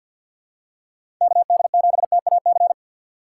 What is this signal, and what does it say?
Morse code, heard as a keyed tone, about 47 wpm, 700 Hz. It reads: KD7TAC
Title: USA Call Signs - 6 Characters - 6x Character Spacing - 50wpm